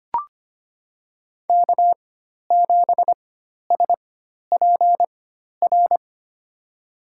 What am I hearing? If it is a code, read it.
Morse code, heard as a keyed tone, about 25 wpm, spaced as for 12 wpm, 700 Hz. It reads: K7SPR
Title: USA Call Signs - 4x Character Spacing - 25wpm